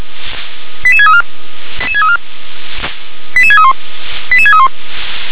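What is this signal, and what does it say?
Five-tone selective call (ZVEI), heard as quick runs of five tones, tones of about 70 ms. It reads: 88634 88631 88631
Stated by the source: ZVEI five-tone selective call received on an SDR.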